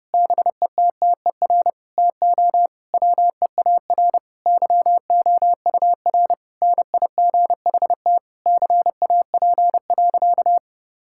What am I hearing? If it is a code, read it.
Morse code, heard as a keyed tone, about 32 wpm, 700 Hz. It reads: BETTER TO WEAR YOUR NIGHT CAP.
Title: Sentences from Top 1600 Words 30wpm